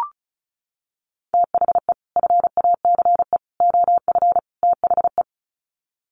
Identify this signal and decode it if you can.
Morse code, heard as a keyed tone, about 34 wpm, 700 Hz. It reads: THE FACE OF THE
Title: Sets of 4 Words 35wpm